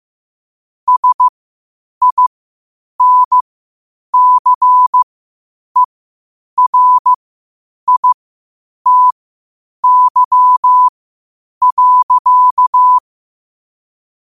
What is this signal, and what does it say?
Morse code, heard as a keyed tone, about 15 wpm, spaced as for 8 wpm, 1000 Hz. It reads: SINCERITY.
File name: qotdfiles / 8wpm